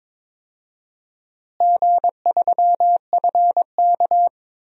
Morse code, heard as a keyed tone, about 22 wpm, 700 Hz. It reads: G3FK
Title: DXCC-100 Call Signs - 4 Characters - Repeated 2x - 22wpm